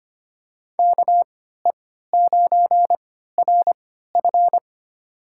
Morse code, heard as a keyed tone, about 25 wpm, spaced as for 14 wpm, 700 Hz. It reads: KE9RF
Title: USA Call Signs - 3x Character Spacing - 25wpm